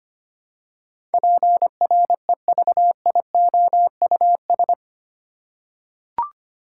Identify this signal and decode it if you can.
Morse code, heard as a keyed tone, about 25 wpm, 690 Hz. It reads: PREVIOUS
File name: 8 Letter Words - First 3 Letters Given - 25wpm